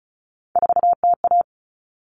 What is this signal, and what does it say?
Morse code, heard as a keyed tone, about 34 wpm, 700 Hz. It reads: 4TA
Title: Common Cut Numbers - 3-Digit 35wpm